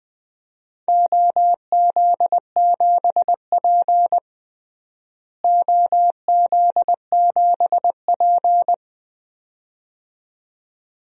Morse code, heard as a keyed tone, about 21 wpm, 700 Hz. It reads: OZ7P OZ7P
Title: DXCC-100 Call Signs - 4 Characters - Repeated 2x - 20wpm